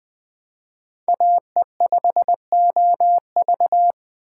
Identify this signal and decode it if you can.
Morse code, heard as a keyed tone, about 20 wpm, 700 Hz. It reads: AE5OV
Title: Speed Racing - USA 2x2 Call Signs 17wpm